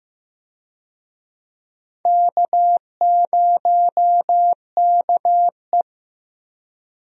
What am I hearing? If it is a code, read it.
Morse code, heard as a keyed tone, about 15 wpm, 700 Hz. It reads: K0KE
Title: USA Call Signs - 4 Characters - Repeated 2x - 15wpm